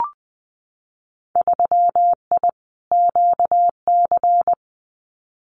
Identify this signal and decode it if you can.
Morse code, heard as a keyed tone, about 20 wpm, 700 Hz. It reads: 3I QC
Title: Winter Field Day Exchanges Encore 20wpm